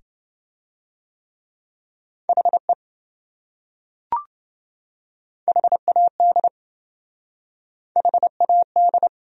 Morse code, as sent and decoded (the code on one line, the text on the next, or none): .... / .... .- -.. / .... .- -..
HE HAD HAD